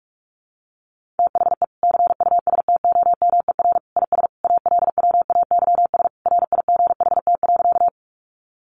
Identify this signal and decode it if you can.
Morse code, heard as a keyed tone, about 46 wpm, 700 Hz. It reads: THE CUSTOMER IS ALWAYS RIGHT.
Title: Sentences from Top 1100 Words 45wpm